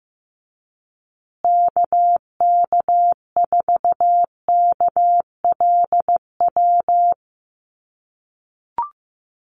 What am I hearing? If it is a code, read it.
Morse code, heard as a keyed tone, about 14 wpm, 700 Hz. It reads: KK4KLW
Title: USA Call Signs Repeated 3x 15wpm